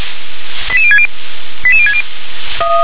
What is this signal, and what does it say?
Five-tone selective call (ZVEI), heard as quick runs of five tones, tones of about 70 ms. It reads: 88060 88060
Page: /357/SDR/ZVEI/88083/